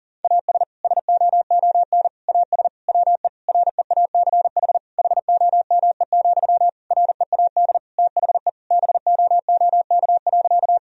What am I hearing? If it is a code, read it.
Morse code, heard as a keyed tone, about 40 wpm, 700 Hz. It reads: AS SOON AS WE REACH HOME, READ THE BOOK.
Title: Sentences from Top 1000 Words - Review Part 2 - 40wpm